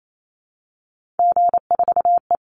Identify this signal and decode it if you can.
Morse code, heard as a keyed tone, about 28 wpm, 700 Hz. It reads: G4E
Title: Call Signs Repeated 2x - 28wpm